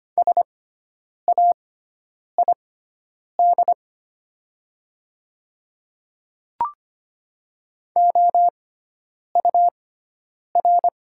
Morse code, heard as a keyed tone, about 25 wpm, spaced as for 9 wpm, 700 Hz. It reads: SAID OUR
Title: 6x Character Spacing - Top 500 Words - 25wpm